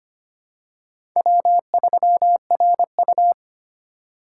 Morse code, heard as a keyed tone, about 25 wpm, 700 Hz. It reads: W3RU
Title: USA Call Signs - 4x Character Spacing - 25wpm